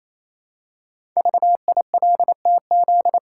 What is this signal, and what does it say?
Morse code, heard as a keyed tone, about 29 wpm, 700 Hz. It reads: VILTZ
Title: Code Groups - 5 Letters - 28wpm